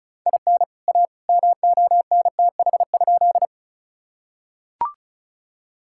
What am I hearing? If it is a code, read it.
Morse code, heard as a keyed tone, about 36 wpm, 700 Hz. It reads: IN A MONTH?
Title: Sentences from Top 1000 Words - Review Part 5 - 35wpm